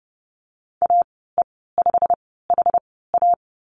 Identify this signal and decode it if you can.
Morse code, heard as a keyed tone, about 30 wpm, spaced as for 17 wpm, 700 Hz. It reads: AE5HA
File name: Lesson 15g - Number 5 - Call Signs 30wpm